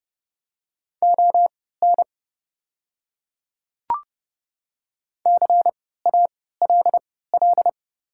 Morse code, heard as a keyed tone, about 30 wpm, spaced as for 17 wpm, 700 Hz. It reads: ON CALL
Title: 3x Character Spacing - Top 100 Words - 30wpm